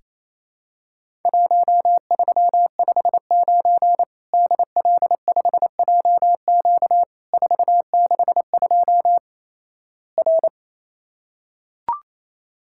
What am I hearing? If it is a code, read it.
Morse code, heard as a keyed tone, about 28 wpm, 700 Hz. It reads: 1359 DL5JQ 462 R
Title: Worked All Europe DX Contest - QTC Exchange - 28wpm